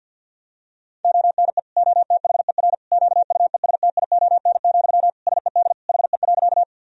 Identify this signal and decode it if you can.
Morse code, heard as a keyed tone, about 51 wpm, 700 Hz. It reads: ONE OTHER QUESTION, SED HE.
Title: Sentences from Top 400 Words and Abbreviations 50wpm